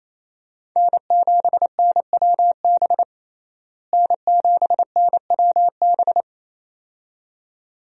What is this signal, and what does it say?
Morse code, heard as a keyed tone, about 27 wpm, 700 Hz. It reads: N7NWB N7NWB
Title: USA Call Signs - 5 Characters - Repeated 2x - 28wpm